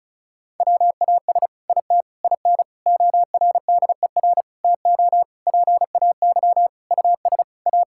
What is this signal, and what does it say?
Morse code, heard as a keyed tone, about 36 wpm, 700 Hz. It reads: WAS IT IN ORDER TO PAY US A
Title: Sentences from Top 1000 Words - Review Part 6 - 35wpm